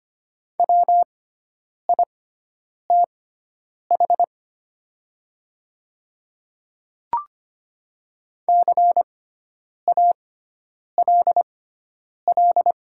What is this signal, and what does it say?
Morse code, heard as a keyed tone, about 25 wpm, spaced as for 9 wpm, 700 Hz. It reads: WITH CALL
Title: 6x Character Spacing - Top 100 Words - 25wpm